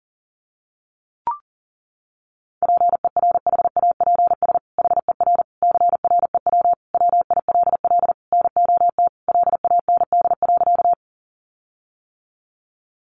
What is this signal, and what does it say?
Morse code, heard as a keyed tone, about 40 wpm, 700 Hz. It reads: PERHAPS HER CREW WILL NOT LAND.
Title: Sentences from Top 1500 Words 40wpm